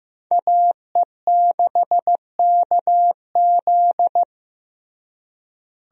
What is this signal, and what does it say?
Morse code, heard as a keyed tone, about 15 wpm, 700 Hz. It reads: AE6KZ
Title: USA Call Signs - 5 Characters - Repeated 2x - 15wpm